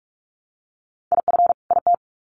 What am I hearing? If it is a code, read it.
Morse code, heard as a keyed tone, about 44 wpm, 700 Hz. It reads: IF IT
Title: Sets of 2 Words 45wpm